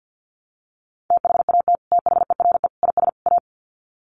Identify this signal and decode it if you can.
Morse code, heard as a keyed tone, about 50 wpm, 700 Hz. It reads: THAT THERE IS A
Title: Sets of 4 Words 50wpm